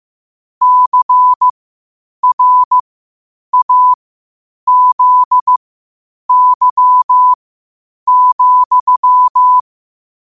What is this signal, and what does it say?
Morse code, heard as a keyed tone, about 15 wpm, spaced as for 8 wpm, 1000 Hz. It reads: CRAZY,
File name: qotdfiles / 8wpm